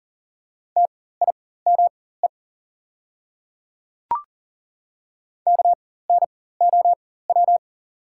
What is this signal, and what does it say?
Morse code, heard as a keyed tone, about 40 wpm, spaced as for 19 wpm, 700 Hz. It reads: TIME KNOW
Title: 4x Character Spacing - Top 100 Words - 40wpm